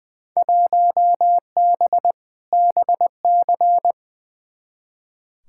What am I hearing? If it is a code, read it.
Morse code, heard as a keyed tone, about 20 wpm, 700 Hz. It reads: 1B BC
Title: Field Day Exchanges - 20wpm